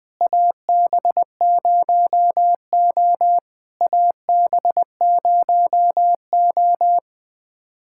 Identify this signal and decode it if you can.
Morse code, heard as a keyed tone, about 19 wpm, 690 Hz. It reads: AB0O AB0O
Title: POTA QSO Practice - 20wpm